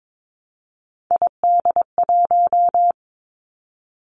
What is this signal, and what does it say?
Morse code, heard as a keyed tone, about 22 wpm, 700 Hz. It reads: ID1
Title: Three Challenging Letter-Number Combinations 22wpm